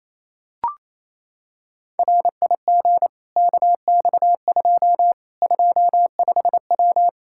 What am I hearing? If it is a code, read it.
Morse code, heard as a keyed tone, about 28 wpm, 700 Hz. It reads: RIG KX2 25W
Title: K1USN SST QSO Practice 28wpm